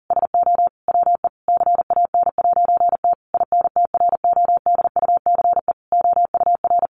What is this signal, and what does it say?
Morse code, heard as a keyed tone, about 40 wpm, 700 Hz. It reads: SO WE CAN'T INTRODUCE OUR
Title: A Sound Of Thunder 40wpm